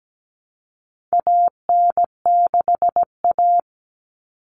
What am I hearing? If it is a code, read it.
Morse code, heard as a keyed tone, about 17 wpm, 700 Hz. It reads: AN6A
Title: All Cut Numbers - 4-Digit 17wpm